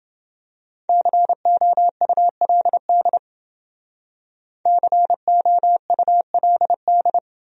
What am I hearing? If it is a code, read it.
Morse code, heard as a keyed tone, about 28 wpm, 700 Hz. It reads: COULD COULD